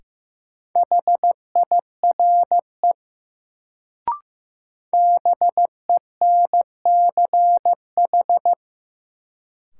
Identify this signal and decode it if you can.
Morse code, heard as a keyed tone, about 15 wpm, 700 Hz. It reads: HIRE BENCH
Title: Sentences from Top 1400 Words - New Word Drill - 15wpm